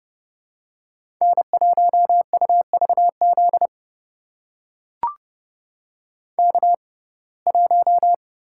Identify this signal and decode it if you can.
Morse code, heard as a keyed tone, about 30 wpm, spaced as for 22 wpm, 700 Hz. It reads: N1UVZ K1
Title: USA Call Signs - 5 Characters - 6x Character Spacing - 30wpm